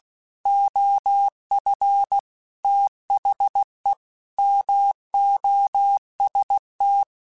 Morse code, heard as a keyed tone, about 16 wpm, 800 Hz. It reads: OF THE MOST